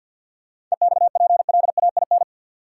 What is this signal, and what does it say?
Morse code, heard as a keyed tone, about 46 wpm, 700 Hz. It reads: EXPLAIN
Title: Sentences from Top 1100 Words - New Word Drill - 50wpm